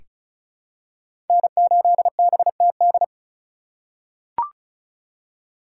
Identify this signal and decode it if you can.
Morse code, heard as a keyed tone, about 36 wpm, 700 Hz. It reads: N8BTD
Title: USA Call Signs - 2x Character Spacing - 35wpm